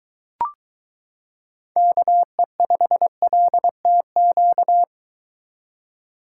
Morse code, heard as a keyed tone, about 22 wpm, 700 Hz. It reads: KE5LTQ